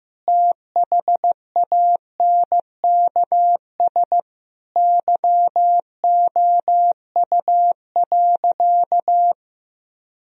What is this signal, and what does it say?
Morse code, heard as a keyed tone, about 15 wpm, 700 Hz. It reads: THANKS YOU.
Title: Sentences from Top 1000 Words - Review Part 3 - 15wpm